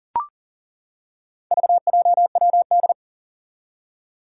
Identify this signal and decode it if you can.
Morse code, heard as a keyed tone, about 37 wpm, 700 Hz. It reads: VJWD
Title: Four Challenging Letter-Number Combinations 40wpm